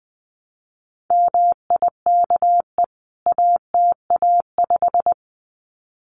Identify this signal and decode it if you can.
Morse code, heard as a keyed tone, about 20 wpm, 700 Hz. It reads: MIKE ATA5
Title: CWT Exchange 20wpm